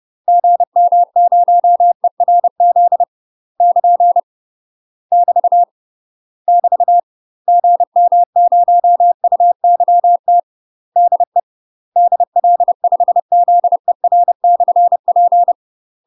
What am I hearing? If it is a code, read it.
Morse code, heard as a keyed tone, about 30 wpm, 700 Hz. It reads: GM0ERZ <KN> <BT> <BT> GM0UYT DE DL5ZER/P
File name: UK MASTER QSOs TRAINING / 30WPM / UK MASTER QSOs 0046